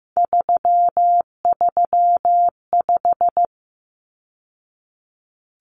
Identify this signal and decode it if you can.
Morse code, heard as a keyed tone, about 15 wpm, 700 Hz. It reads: 335
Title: Numbers - 3-Digit 15wpm